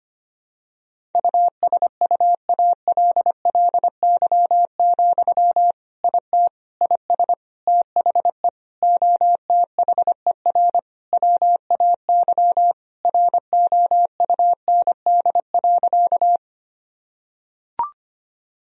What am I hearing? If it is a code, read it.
Morse code, heard as a keyed tone, about 25 wpm, 700 Hz. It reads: USUALLY, IT IS THE OTHER WAY ROUND.
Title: Sentences from Top 1000 Words - Review Part 3 - 25wpm